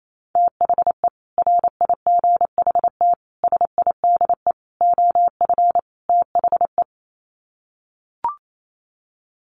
Morse code, heard as a keyed tone, about 28 wpm, 700 Hz. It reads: THE RIGHT SIDE OF THE
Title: Sets of 5 Words Encore 28wpm